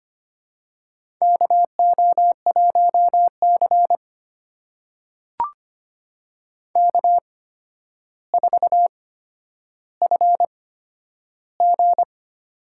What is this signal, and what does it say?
Morse code, heard as a keyed tone, about 25 wpm, spaced as for 11 wpm, 700 Hz. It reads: KO1C K4FG
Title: USA Call Signs - 4 Characters - 8x Character Spacing - 25wpm